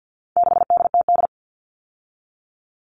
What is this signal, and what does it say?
Morse code, heard as a keyed tone, about 50 wpm, 700 Hz. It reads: 6DTD
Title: All Cut Numbers - 4-Digit 50wpm